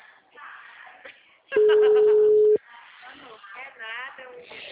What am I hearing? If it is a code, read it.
Ringback tone (Europe, 425 Hz), with speech.
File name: out- -2029- - - .67723